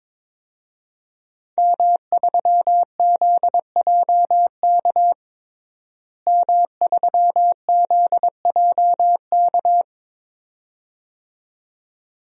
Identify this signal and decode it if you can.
Morse code, heard as a keyed tone, about 22 wpm, 700 Hz. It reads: M3ZJK M3ZJK
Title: DXCC-100 Call Signs - 5 Characters - Repeated 2x - 22wpm